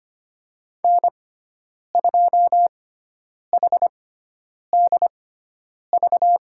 Morse code, heard as a keyed tone, about 25 wpm, spaced as for 9 wpm, 700 Hz. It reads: N2HDV
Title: USA Call Signs - 5 Characters - 6x Character Spacing - 25wpm